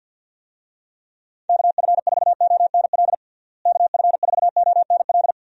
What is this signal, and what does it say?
Morse code, heard as a keyed tone, about 49 wpm, 700 Hz. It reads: KF4ONL KF4ONL
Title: USA Call Signs - 6 Characters - Repeated 2x - 50wpm